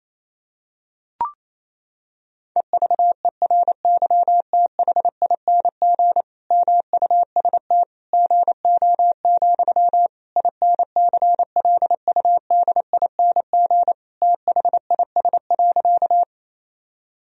Morse code, heard as a keyed tone, about 28 wpm, 700 Hz. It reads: EVERYTHING MUST GO, INCLUDING THIS.
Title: Sentences from Top 1000 Words - Review Part 4 - 28wpm